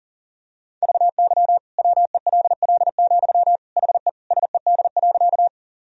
Morse code, heard as a keyed tone, about 37 wpm, 700 Hz. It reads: VY WELL, HE SED.
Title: Sentences from Top 200 Words and Abbreviations 40wpm